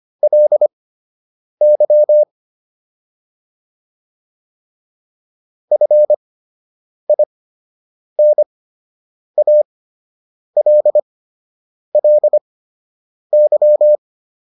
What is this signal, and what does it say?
Morse code, heard as a keyed tone, about 25 wpm, spaced as for 8 wpm, 600 Hz. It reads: LY FINALLY